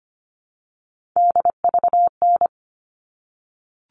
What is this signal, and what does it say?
Morse code, heard as a keyed tone, about 25 wpm, 700 Hz. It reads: DVN
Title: All Cut Numbers - 3-Digit 25wpm